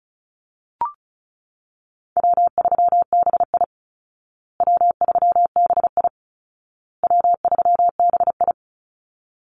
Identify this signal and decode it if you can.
Morse code, heard as a keyed tone, about 34 wpm, 700 Hz. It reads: W3BI W3BI W3BI
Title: Easy Call Signs Repeated 3x - 35wpm